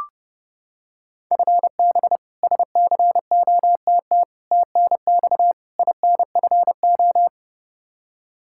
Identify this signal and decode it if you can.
Morse code, heard as a keyed tone, about 30 wpm, 700 Hz. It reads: FB SCOTT TNX INFO
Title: K1USN SST QSO Practice 30wpm